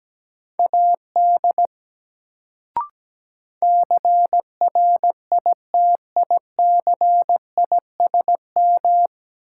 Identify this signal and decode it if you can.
Morse code, heard as a keyed tone, about 17 wpm, 700 Hz. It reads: AD CRITICISM
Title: Sentences from Top 1300 Words - New Word Drill Alt - 17wpm